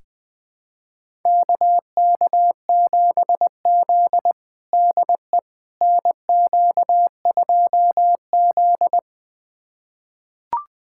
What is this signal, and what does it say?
Morse code, heard as a keyed tone, about 20 wpm, 700 Hz. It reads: KK7Z DE NQ2Z
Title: USA Callsign DE Callsign - 4 Characters - 20wpm